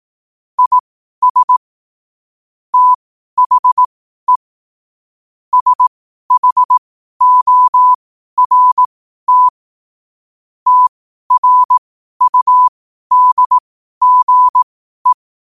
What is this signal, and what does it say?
Morse code, heard as a keyed tone, about 18 wpm, spaced as for 12 wpm, 1000 Hz. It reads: IS THE SHORT TRUDGE
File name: qotdfiles / 12wpm